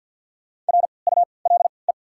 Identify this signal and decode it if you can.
Morse code, heard as a keyed tone, about 53 wpm, spaced as for 29 wpm, 690 Hz. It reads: RULE
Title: 3x Character Spacing - Top 500 Words - 50wpm